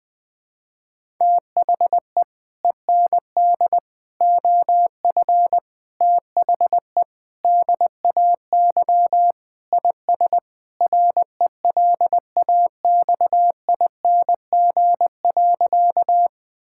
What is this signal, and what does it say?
Morse code, heard as a keyed tone, about 20 wpm, 700 Hz. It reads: THE END OF THE DAY IS RELAXING.
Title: Sentences with 3-Word Phrases - Spoken Beginning - 20wpm